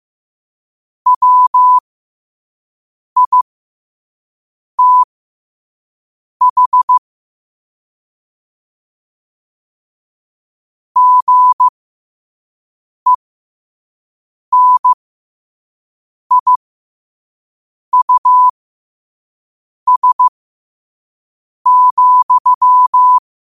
Morse code, heard as a keyed tone, about 15 wpm, spaced as for 5 wpm, 1000 Hz. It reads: WITH GENIUS,
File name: qotdfiles / 5wpm